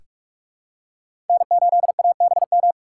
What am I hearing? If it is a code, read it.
Morse code, heard as a keyed tone, about 44 wpm, 700 Hz. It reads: N8ABM
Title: USA Call Signs - 5 Characters - 6x Character Spacing - 45wpm